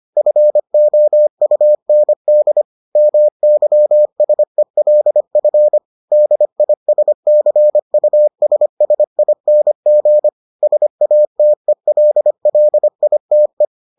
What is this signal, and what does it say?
Morse code, heard as a keyed tone, about 25 wpm, 600 Hz. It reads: FOUND MYSELF DISCUSSING SATELLITE